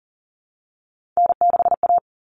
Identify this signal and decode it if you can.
Morse code, heard as a keyed tone, about 40 wpm, 700 Hz. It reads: N6A
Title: All Cut Numbers - 3-Digit 40wpm